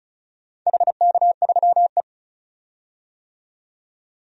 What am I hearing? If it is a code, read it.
Morse code, heard as a keyed tone, about 36 wpm, 700 Hz. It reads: HK3E
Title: DXCC-100 Call Signs - 4 Characters - 35wpm